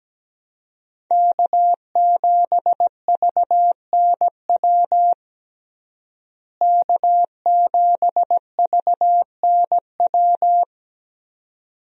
Morse code, heard as a keyed tone, about 17 wpm, 700 Hz. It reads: K7VNW K7VNW
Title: USA Call Signs - 5 Characters - Repeated 2x - 17wpm